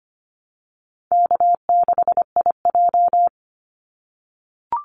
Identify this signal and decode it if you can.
Morse code, heard as a keyed tone, about 25 wpm, 700 Hz. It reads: K6IJ
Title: Easy Call Signs - 25wpm